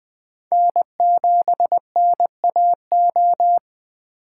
Morse code, heard as a keyed tone, about 23 wpm, 700 Hz. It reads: N7NAO
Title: Speed Racing - USA 1x3 Call Signs 20wpm